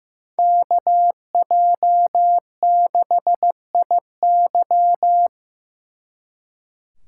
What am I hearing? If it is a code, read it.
Morse code, heard as a keyed tone, about 18 wpm, 700 Hz. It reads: KJ6IY
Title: Speed Racing - USA 2x2 Call Signs 15wpm